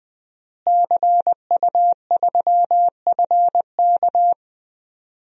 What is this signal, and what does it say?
Morse code, heard as a keyed tone, about 21 wpm, 700 Hz. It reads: CU3FK
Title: DXCC-100 Call Signs - 5 Characters - Repeated 3x - 20wpm